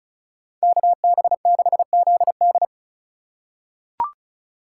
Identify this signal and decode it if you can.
Morse code, heard as a keyed tone, about 36 wpm, 700 Hz. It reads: KB6ZD
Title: USA Call Signs - 4x Character Spacing - 35wpm